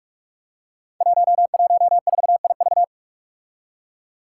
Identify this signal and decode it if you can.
Morse code, heard as a keyed tone, about 44 wpm, 700 Hz. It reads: 114IV
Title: Five Challenging Letter-Number Combinations 45wpm